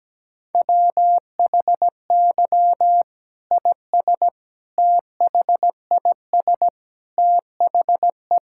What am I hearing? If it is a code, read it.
Morse code, heard as a keyed tone, about 17 wpm, 700 Hz. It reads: WHY IS THIS THE
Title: Sentences from Top 1000 Words - Review Part 5 - 17wpm